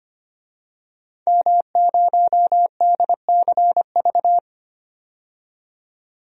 Morse code, heard as a keyed tone, about 25 wpm, 700 Hz. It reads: M0DCV
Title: DXCC-100 Call Signs - 5 Characters - 25wpm